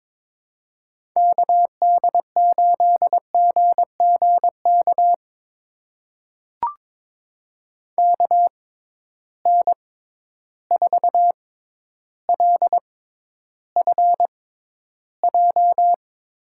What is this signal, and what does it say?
Morse code, heard as a keyed tone, about 22 wpm, spaced as for 11 wpm, 700 Hz. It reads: KD8GGK KN4LFJ
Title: USA Call Signs - 6 Characters - 6x Character Spacing - 22wpm